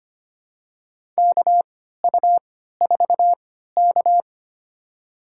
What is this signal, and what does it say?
Morse code, heard as a keyed tone, about 25 wpm, spaced as for 14 wpm, 700 Hz. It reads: KU4K
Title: USA Call Signs - 3x Character Spacing - 25wpm